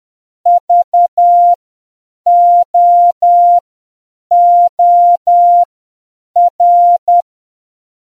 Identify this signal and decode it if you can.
Morse code, heard as a keyed tone, about 10 wpm, spaced as for 7 wpm, 700 Hz. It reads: VOOR